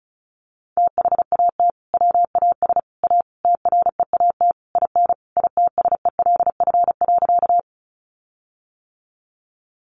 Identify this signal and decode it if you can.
Morse code, heard as a keyed tone, about 34 wpm, 700 Hz. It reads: THAT WAS A TREAT IN ITSELF.
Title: Sentences from Top 1300 Words 35wpm